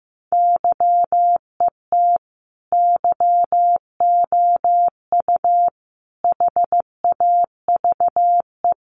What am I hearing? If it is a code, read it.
Morse code, heard as a keyed tone, about 15 wpm, 700 Hz. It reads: YET YOU HAVE
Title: Sentences from Top 600 Words 15wpm